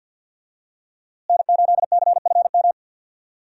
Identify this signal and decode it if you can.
Morse code, heard as a keyed tone, about 51 wpm, 700 Hz. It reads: N8CFM
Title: USA Call Signs - 5 Characters - Repeated 3x - 50wpm